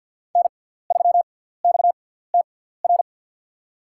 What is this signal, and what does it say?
Morse code, heard as a keyed tone, about 50 wpm, spaced as for 17 wpm, 700 Hz. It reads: N3XTR
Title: USA Call Signs - 5 Characters - 6x Character Spacing - 50wpm